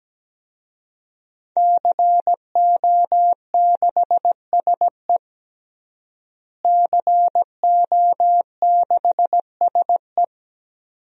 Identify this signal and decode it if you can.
Morse code, heard as a keyed tone, about 17 wpm, 700 Hz. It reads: CO6SE CO6SE
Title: DXCC-100 Call Signs - 5 Characters - Repeated 2x - 17wpm